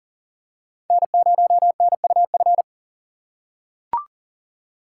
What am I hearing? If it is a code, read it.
Morse code, heard as a keyed tone, about 39 wpm, 700 Hz. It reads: N0NUF
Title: USA Call Signs - 5 Characters - 4x Character Spacing - 40wpm